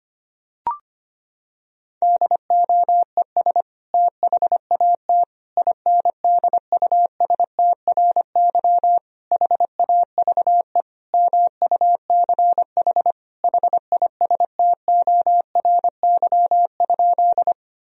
Morse code, heard as a keyed tone, about 25 wpm, 700 Hz. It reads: DOES THAT INDUSTRY HAVE MUCH HISTORY?
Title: Sentences from Top 1000 Words - Review Part 4 - 25wpm